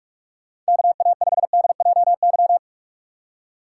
Morse code, heard as a keyed tone, about 46 wpm, 700 Hz. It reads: KA5DJY
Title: USA Call Signs - 6x Character Spacing - 45wpm